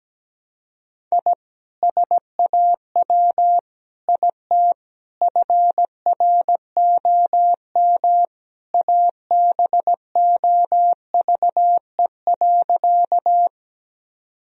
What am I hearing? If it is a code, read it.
Morse code, heard as a keyed tone, about 17 wpm, 700 Hz. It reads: I SAW IT FROM ABOVE.
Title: Sentences from Top 1000 Words - Review Part 2 - 17wpm